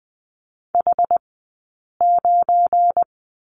Morse code, heard as a keyed tone, about 20 wpm, spaced as for 8 wpm, 700 Hz. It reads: H9
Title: 197 Lesson 32d - Letter k - Two Characters 20wpm